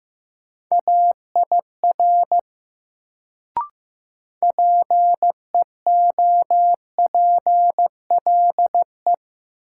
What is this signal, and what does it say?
Morse code, heard as a keyed tone, about 15 wpm, 700 Hz. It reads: AIR PEOPLE
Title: Sentences from Top 200 Words - New Word Drill - 15wpm